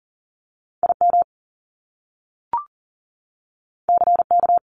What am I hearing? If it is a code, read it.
Morse code, heard as a keyed tone, about 42 wpm, 700 Hz. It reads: IM CK
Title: Top Two Letter Combinations in Words 40wpm